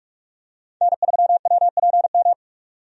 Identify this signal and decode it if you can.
Morse code, heard as a keyed tone, about 46 wpm, 700 Hz. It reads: N3WPM
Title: USA Call Signs - 5 Characters - 45wpm